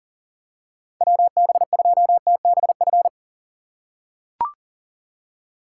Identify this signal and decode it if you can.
Morse code, heard as a keyed tone, about 40 wpm, 690 Hz. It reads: WB2TBF
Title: USA Call Signs - 8x Character Spacing - 40wpm